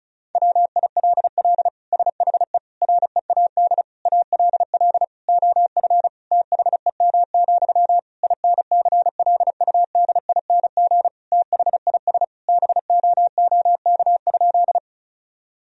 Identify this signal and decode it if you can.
Morse code, heard as a keyed tone, about 34 wpm, 700 Hz. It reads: WILL SHE READ ALL OF THEM, INCLUDING THIS BOOK?
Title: Sentences from Top 1000 Words - Review Part 4 - 35wpm